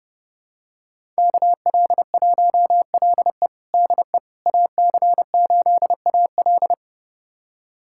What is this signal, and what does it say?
Morse code, heard as a keyed tone, about 30 wpm, 700 Hz. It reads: KL1LE DE AC8AL
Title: USA Callsign DE Callsign - 5 Characters - 30wpm